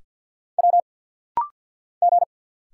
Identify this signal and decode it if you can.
Morse code, heard as a keyed tone, about 50 wpm, 700 Hz. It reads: W G
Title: Single Letters - Rapid-Fire 50wpm